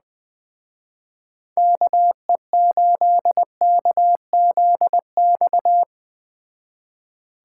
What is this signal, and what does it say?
Morse code, heard as a keyed tone, about 20 wpm, 700 Hz. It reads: KE8KZX